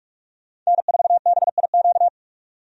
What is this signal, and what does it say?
Morse code, heard as a keyed tone, about 47 wpm, 700 Hz. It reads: N4BIQ
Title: USA Call Signs - 5 Characters - 45wpm